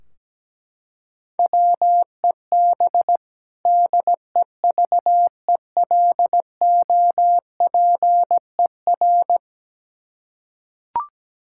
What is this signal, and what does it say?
Morse code, heard as a keyed tone, about 17 wpm, 700 Hz. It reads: WEB DEVELOPER
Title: Practice QSO - Exchange 3 - More Info - 17wpm